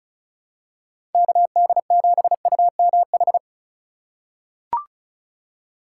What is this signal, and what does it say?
Morse code, heard as a keyed tone, about 34 wpm, 700 Hz. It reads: KD7UMH
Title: USA Call Signs - 6 Characters - 4x Character Spacing - 35wpm